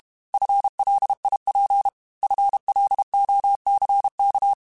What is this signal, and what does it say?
Morse code, heard as a keyed tone, about 32 wpm, 800 Hz. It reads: FLIP FLOCK
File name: FL Runs / FL Runs mp3s / FL Runs @30